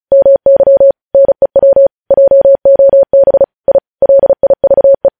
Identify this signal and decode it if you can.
Morse code, heard as a keyed tone, about 36 wpm, 550 Hz. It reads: MY NEW JOB I LIVE
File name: podcast / News headlines-1x-35x35